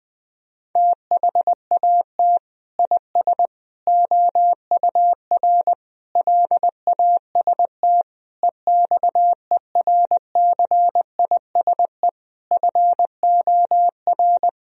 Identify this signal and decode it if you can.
Morse code, heard as a keyed tone, about 19 wpm, 700 Hz. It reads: THAT IS OUR LAST EXERCISE FOR
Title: Sentences from Top 1000 Words - Review Part 4 - 20wpm